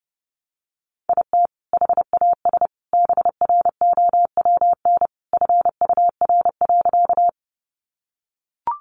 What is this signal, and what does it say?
Morse code, heard as a keyed tone, about 30 wpm, 700 Hz. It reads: IT HAS BROWN FUR.
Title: Easy Phrases 30wpm